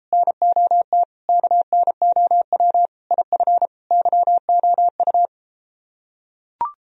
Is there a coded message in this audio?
Yes: Morse code from 0.1 to 5.3 s, heard as a keyed tone, about 33 wpm, 700 Hz. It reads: NOT KNOW IF YOU